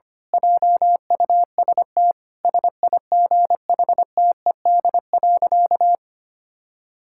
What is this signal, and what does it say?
Morse code, heard as a keyed tone, about 25 wpm, 700 Hz. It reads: JUST SIGHTED.
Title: Pangram Challenge v2 - 25wpm